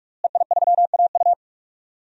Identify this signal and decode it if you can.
Morse code, heard as a keyed tone, about 48 wpm, 700 Hz. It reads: EI3AU
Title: DXCC-100 Call Signs - 5 Characters - Repeated 2x - 45wpm